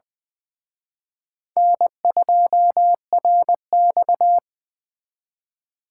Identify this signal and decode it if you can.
Morse code, heard as a keyed tone, about 23 wpm, 700 Hz. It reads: N2RX
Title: Speed Racing - USA 1x2 Call Signs 20wpm